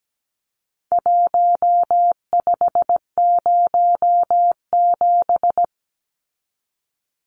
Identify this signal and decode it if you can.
Morse code, heard as a keyed tone, about 17 wpm, 700 Hz. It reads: 1507
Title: Numbers - Contest Style - 4-Digit 17wpm